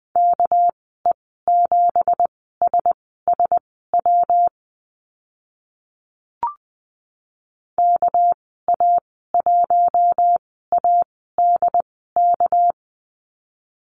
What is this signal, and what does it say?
Morse code, heard as a keyed tone, about 20 wpm, spaced as for 14 wpm, 700 Hz. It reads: KE7SSW KA1ADK
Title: 213 Lesson 34g - Number 8 - Call Signs 20wpm